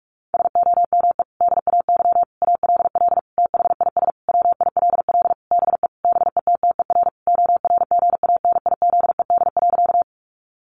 Morse code, heard as a keyed tone, about 44 wpm, 700 Hz. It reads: SOME DAY ALL THIS WILL BE BETTER ORGANIZED.
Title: Sentences from Top 1300 Words 45wpm